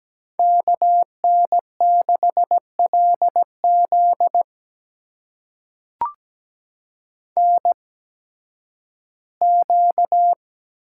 Morse code, heard as a keyed tone, about 17 wpm, spaced as for 11 wpm, 700 Hz. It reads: KN6LZ NQ
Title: USA Call Signs - 8x Character Spacing - 17wpm